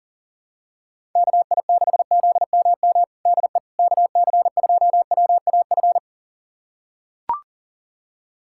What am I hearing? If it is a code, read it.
Morse code, heard as a keyed tone, about 37 wpm, 700 Hz. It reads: KI6ZMM DE KC2WAF
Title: USA Callsign DE Callsign 40wpm